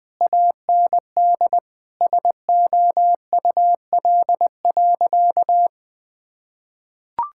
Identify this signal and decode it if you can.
Morse code, heard as a keyed tone, about 20 wpm, 700 Hz. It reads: AND SOUL.
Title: Sentences with 9-Word Phrases - Spoken Beginning - 20wpm